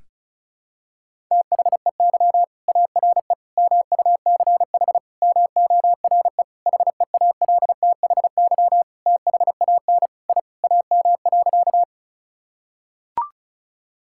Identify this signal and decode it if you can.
Morse code, heard as a keyed tone, about 36 wpm, 700 Hz. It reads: THEY ARE MUCH MORE HEALTHY THAN I AM.
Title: Sentences from Top 1000 Words - Review Part 5 - 35wpm